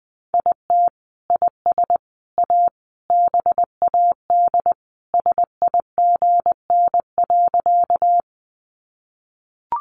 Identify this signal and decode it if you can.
Morse code, heard as a keyed tone, about 20 wpm, 700 Hz. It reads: IT IS A BAD SIGN.
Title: Sentences from Top 1000 Words 20wpm